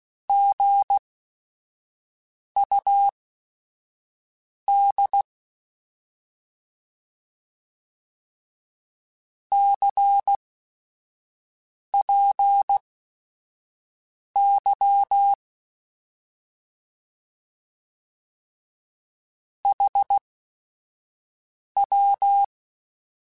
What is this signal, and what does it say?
Morse code, heard as a keyed tone, about 16 wpm, spaced as for 5 wpm, 800 Hz. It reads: GUD CPY HW
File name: Short Runs / Short Runs-mp3s / Words @5